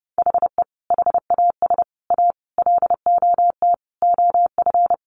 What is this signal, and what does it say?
Morse code, heard as a keyed tone, about 30 wpm, 700 Hz. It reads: HE HAS A LOT OF
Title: Additional Easy Phrases 30wpm